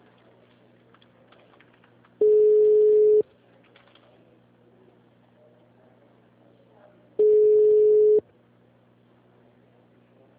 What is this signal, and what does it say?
Ringback tone (Europe, 425 Hz).
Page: Marilia - SP